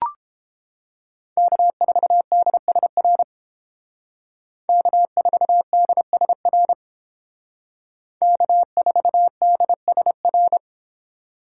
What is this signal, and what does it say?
Morse code, heard as a keyed tone, about 30 wpm, 700 Hz. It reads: K4DSR K4DSR K4DSR